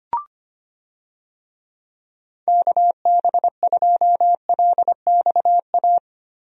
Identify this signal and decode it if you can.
Morse code, heard as a keyed tone, about 25 wpm, 700 Hz. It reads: KB2LXA
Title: USA Call Signs - 6 Characters - Repeated 3x - 25wpm